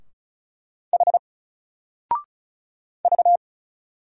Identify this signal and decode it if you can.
Morse code, heard as a keyed tone, about 33 wpm, 700 Hz. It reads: H V
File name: Single-Letter-Number-35wpm